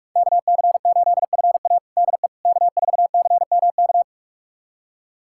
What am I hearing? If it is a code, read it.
Morse code, heard as a keyed tone, about 46 wpm, 700 Hz. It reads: KC8FA DE K4CMK
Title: USA Callsign DE Callsign - 5 Characters - 45wpm